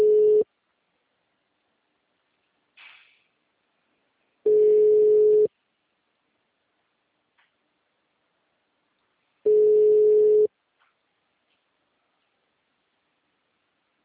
Ringback tone (Europe, 425 Hz).